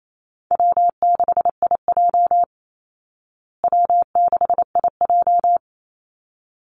Morse code, heard as a keyed tone, about 28 wpm, 700 Hz. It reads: W6IJ W6IJ
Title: Easy Call Signs Repeated 2x - 28wpm